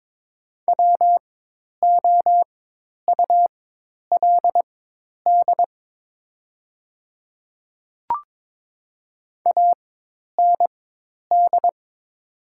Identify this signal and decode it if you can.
Morse code, heard as a keyed tone, about 22 wpm, spaced as for 10 wpm, 700 Hz. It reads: WOULD AND
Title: 4x Character Spacing - Top 300 Words - 22wpm